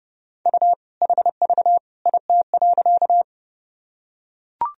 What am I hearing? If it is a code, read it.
Morse code, heard as a keyed tone, about 32 wpm, 700 Hz. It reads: U HV IT.
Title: Sentences from Top 100 Words and Abbreviations 30wpm